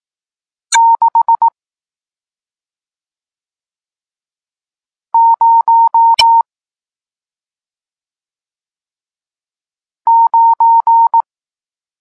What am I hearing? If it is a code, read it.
Morse code, heard as a keyed tone, about 18 wpm, 920 Hz. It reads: 6 0 9
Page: Tekst 40 – Nauka znaków przestankowych